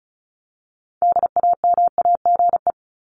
Morse code, heard as a keyed tone, about 34 wpm, 700 Hz. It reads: DAMAGE
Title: Top 1001-1250 Words 35wpm